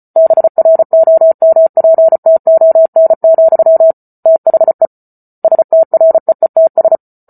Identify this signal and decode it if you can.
Morse code, heard as a keyed tone, about 34 wpm, 650 Hz. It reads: BROMPTON, THE STREETS